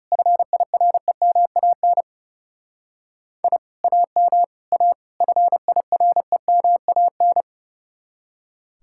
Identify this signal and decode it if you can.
Morse code, heard as a keyed tone, about 38 wpm, 700 Hz. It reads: FIREMAN I AM A FIREMAN